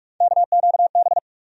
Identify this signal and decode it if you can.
Morse code, heard as a keyed tone, about 46 wpm, 700 Hz. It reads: KQB
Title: USA Call Sign Postfixes - 3 Letters 45wpm